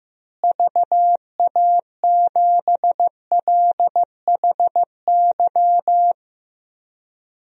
Morse code, heard as a keyed tone, about 16 wpm, 700 Hz. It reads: VA7LHY
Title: DXCC-100 Call Signs - 6 Characters - Repeated 3x - 15wpm